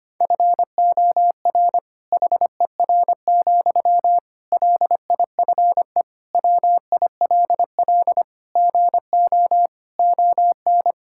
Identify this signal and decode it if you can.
Morse code, heard as a keyed tone, about 25 wpm, 700 Hz. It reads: FOR HER, LIFE WILL GO ON
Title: Sentences from Top 1600 Words 25wpm